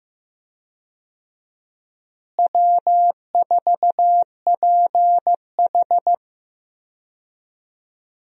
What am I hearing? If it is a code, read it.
Morse code, heard as a keyed tone, about 15 wpm, 700 Hz. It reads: W4PH
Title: USA Call Signs - 4 Characters - Repeated 2x - 15wpm